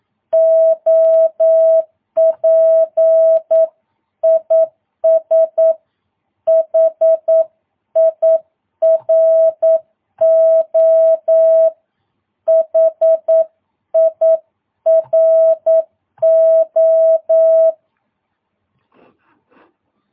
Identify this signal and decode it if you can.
Morse code, heard as a keyed tone, about 9 wpm, 650 Hz. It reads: OP IS HIRO HIRO
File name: jh8cbh / 2022cw